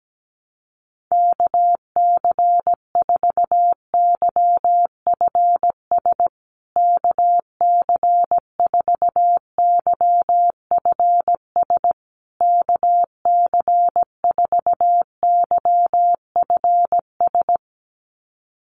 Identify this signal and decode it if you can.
Morse code, heard as a keyed tone, about 17 wpm, 700 Hz. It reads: KC4YFS KC4YFS KC4YFS
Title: USA Call Signs Repeated 3x 17wpm